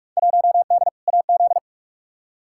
Morse code, heard as a keyed tone, about 46 wpm, 700 Hz. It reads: 1D AZ
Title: Field Day Exchanges - 45wpm